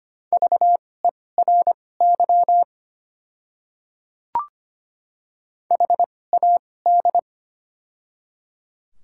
Morse code, heard as a keyed tone, about 25 wpm, spaced as for 18 wpm, 700 Hz. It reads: VERY HAD
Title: 2x Character Spacing - Top 500 Words - 25wpm